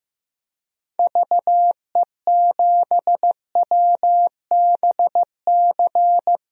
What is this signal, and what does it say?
Morse code, heard as a keyed tone, about 15 wpm, 700 Hz. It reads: VE7WBC
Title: DXCC-100 Call Signs - 6 Characters - 15wpm